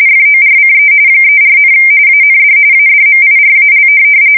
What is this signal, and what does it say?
RTTY, about 45.45 baud. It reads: FORCE 34, AND ADMIRAL
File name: fsk170 / USS-New-Jersey